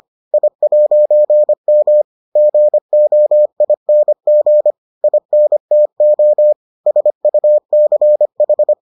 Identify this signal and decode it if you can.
Morse code, heard as a keyed tone, about 25 wpm, 600 Hz. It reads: I'M GOING INTO SUCH